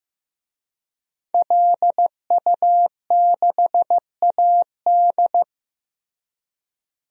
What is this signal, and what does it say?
Morse code, heard as a keyed tone, about 15 wpm, 700 Hz. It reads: LU6AD
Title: DXCC-100 Call Signs - 5 Characters - Repeated 3x - 15wpm